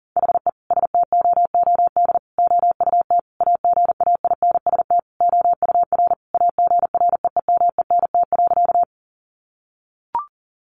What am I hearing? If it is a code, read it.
Morse code, heard as a keyed tone, about 40 wpm, 700 Hz. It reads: HE STOOD OUT AGAINST OUR AGREEMENT.
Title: Sentences from Top 1500 Words 40wpm